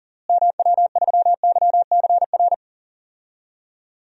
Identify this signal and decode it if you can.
Morse code, heard as a keyed tone, about 40 wpm, 700 Hz. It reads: MW3YCR
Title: DXCC-100 Call Signs - 6 Characters - Repeated 3x - 40wpm